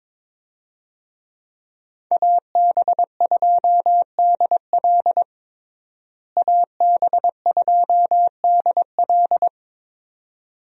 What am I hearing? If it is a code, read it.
Morse code, heard as a keyed tone, about 22 wpm, 700 Hz. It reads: AB2DL AB2DL
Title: USA Call Signs - 5 Characters - Repeated 2x - 22wpm